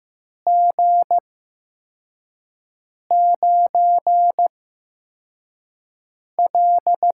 Morse code, heard as a keyed tone, about 15 wpm, spaced as for 4 wpm, 700 Hz. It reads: G9L
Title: USA Call Signs - 4 Characters - 8x Character Spacing - 15wpm